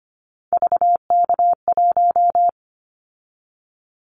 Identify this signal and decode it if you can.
Morse code, heard as a keyed tone, about 25 wpm, 700 Hz. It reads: VK1
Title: Callsign Prefixes for Top 100 DXCC - 25wpm